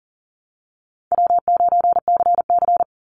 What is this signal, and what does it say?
Morse code, heard as a keyed tone, about 40 wpm, 700 Hz. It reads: W9CC
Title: Call Signs Repeated 2x - 40wpm